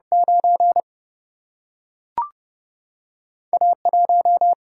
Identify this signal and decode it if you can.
Morse code, heard as a keyed tone, about 31 wpm, 700 Hz. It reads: 9 A 1
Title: Single-Letter Single-Number 30wpm